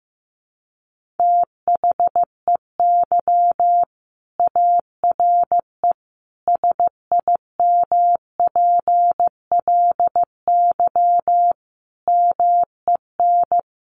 Morse code, heard as a keyed tone, about 15 wpm, 700 Hz. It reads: THEY ARE SIMPLY MEN
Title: Sentences from Top 1500 Words 15wpm